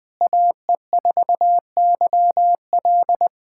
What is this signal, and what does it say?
Morse code, heard as a keyed tone, about 20 wpm, 700 Hz. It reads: AE4YL
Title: USA Callsign DE Callsign - 5 Characters - 20wpm